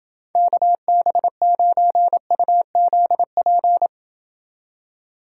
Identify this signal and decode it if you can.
Morse code, heard as a keyed tone, about 29 wpm, 700 Hz. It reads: KB9UZP